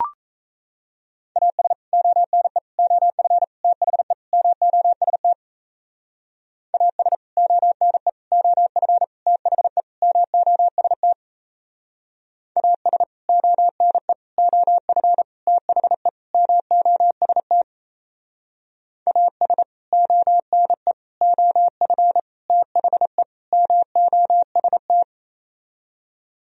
Morse code, heard as a keyed tone, about 36 wpm, 700 Hz. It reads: AS ONE OF THE MOST AS ONE OF THE MOST AS ONE OF THE MOST AS ONE OF THE MOST